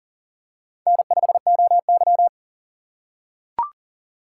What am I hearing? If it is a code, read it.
Morse code, heard as a keyed tone, about 40 wpm, 700 Hz. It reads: N5OY
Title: USA Call Signs - 4 Characters - 3x Character Spacing - 40wpm